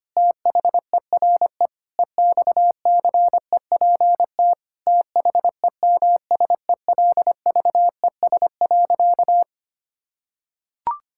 Morse code, heard as a keyed tone, about 25 wpm, 700 Hz. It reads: THERE EXCEPT THEMSELVES.
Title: Sentences from Top 1600 Words 25wpm